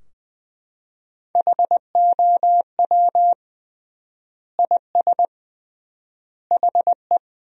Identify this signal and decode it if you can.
Morse code, heard as a keyed tone, about 20 wpm, 700 Hz. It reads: HOW IS HE
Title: 3x Word-Spacing - Sentences from Top 100 Words - 20wpm